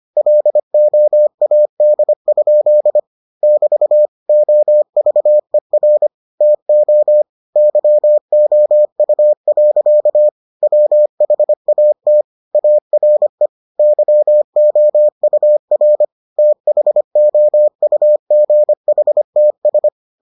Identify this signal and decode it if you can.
Morse code, heard as a keyed tone, about 25 wpm, 600 Hz. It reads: LOAD? <BT> OVER TO YOU. WHAT ARE YOUR THOUGHTS